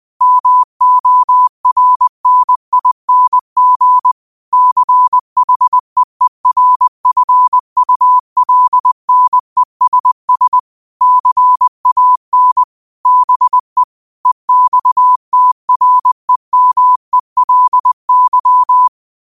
Morse code, heard as a keyed tone, about 20 wpm, 1000 Hz. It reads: MORNING CHEERFULNESS CAN BE EXTREMELY
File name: qotdfiles / 20wpm